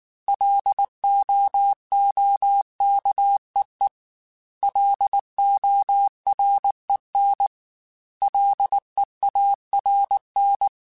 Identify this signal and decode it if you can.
Morse code, heard as a keyed tone, about 19 wpm, 800 Hz. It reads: LOOKEE LOREN LEARN